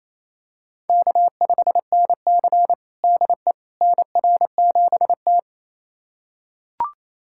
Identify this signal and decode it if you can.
Morse code, heard as a keyed tone, about 29 wpm, 700 Hz. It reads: K5NC DE NR7T
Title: USA Callsign DE Callsign - 4 Characters - 28wpm